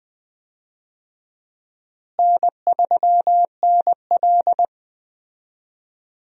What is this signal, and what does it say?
Morse code, heard as a keyed tone, about 20 wpm, 700 Hz. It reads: N3NL
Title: USA Call Signs - 4 Characters - Repeated 2x - 20wpm